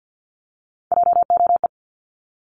Morse code, heard as a keyed tone, about 50 wpm, 700 Hz. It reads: JOE
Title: Multi-letter Phonemes 50wpm